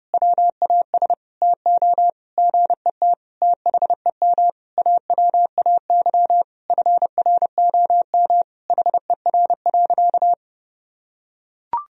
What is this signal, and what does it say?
Morse code, heard as a keyed tone, about 30 wpm, 700 Hz. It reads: WAS TO GET THEM AWAY FROM HER.
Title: Sentences from Top 1000 Words - Review Part 2 - 30wpm